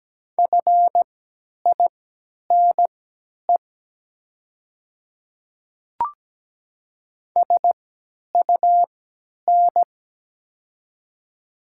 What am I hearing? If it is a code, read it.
Morse code, heard as a keyed tone, about 17 wpm, spaced as for 10 wpm, 690 Hz. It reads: FINE SUN
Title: 3x Character Spacing - Top 500 Words - 17wpm